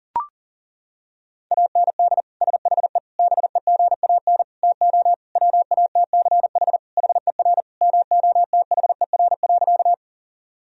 Morse code, heard as a keyed tone, about 40 wpm, 700 Hz. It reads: AND SHE BEGAN TO WATCH HER MOTHER.
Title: Sentences from Top 1000 Words - Review Part 1 - 40wpm